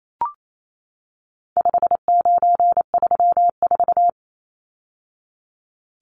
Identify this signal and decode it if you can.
Morse code, heard as a keyed tone, about 28 wpm, 700 Hz. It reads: 5934
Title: Numbers - 4-Digit 28wpm